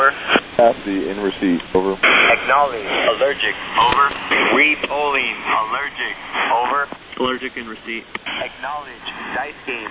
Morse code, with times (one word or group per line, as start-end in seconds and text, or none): none